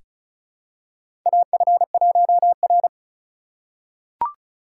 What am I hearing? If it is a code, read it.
Morse code, heard as a keyed tone, about 34 wpm, 700 Hz. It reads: AF1R
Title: USA Call Signs - 4 Characters - 2x Character Spacing - 35wpm